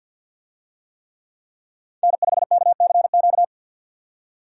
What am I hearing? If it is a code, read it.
Morse code, heard as a keyed tone, about 51 wpm, 700 Hz. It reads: N5KCQ
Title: USA Call Signs - 5 Characters - 50wpm